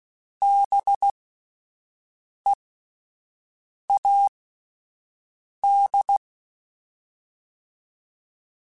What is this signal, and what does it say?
Morse code, heard as a keyed tone, about 16 wpm, spaced as for 5 wpm, 800 Hz. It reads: BEAD